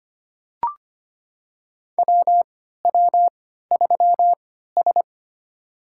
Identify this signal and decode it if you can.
Morse code, heard as a keyed tone, about 25 wpm, spaced as for 14 wpm, 700 Hz. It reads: WW3S
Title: USA Call Signs - 4 Characters - 3x Character Spacing - 25wpm